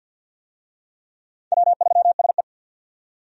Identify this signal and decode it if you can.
Morse code, heard as a keyed tone, about 50 wpm, 700 Hz. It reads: W3SE
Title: USA Call Signs - 4 Characters - 50wpm